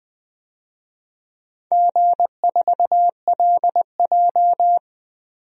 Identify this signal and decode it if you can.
Morse code, heard as a keyed tone, about 20 wpm, 700 Hz. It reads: G4LJ
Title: DXCC-100 Call Signs - 4 Characters - 20wpm